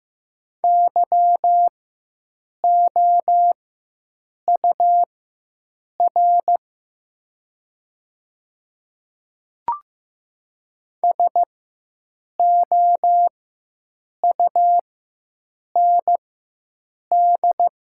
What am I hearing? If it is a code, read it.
Morse code, heard as a keyed tone, about 15 wpm, spaced as for 7 wpm, 700 Hz. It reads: YOUR SOUND
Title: 4x Character Spacing - Top 100 Words - 15wpm